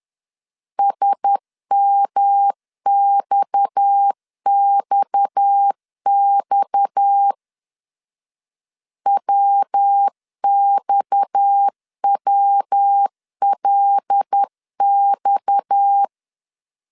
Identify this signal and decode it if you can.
Morse code, heard as a keyed tone, about 10 wpm, 800 Hz. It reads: SMXXX WXWLX
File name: LEZIONI / 6 PKX0 / 6 PKX 10wpm 2426